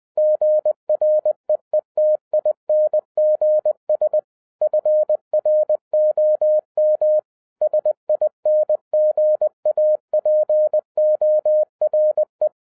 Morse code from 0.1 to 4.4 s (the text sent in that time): GREETINGS